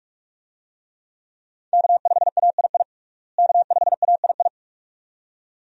Morse code, heard as a keyed tone, about 47 wpm, 700 Hz. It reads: K5AII K5AII
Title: USA Call Signs - 5 Characters - Repeated 2x - 45wpm